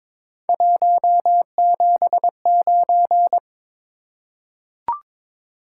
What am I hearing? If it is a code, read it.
Morse code, heard as a keyed tone, about 22 wpm, 700 Hz. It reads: 179
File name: practice / 22wpm / Three Letter-Number Combinations - 22wpm